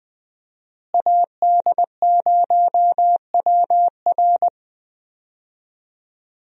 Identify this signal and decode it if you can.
Morse code, heard as a keyed tone, about 20 wpm, 700 Hz. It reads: AD0WR
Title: USA Call Signs - 5 Characters - Repeated 3x - 20wpm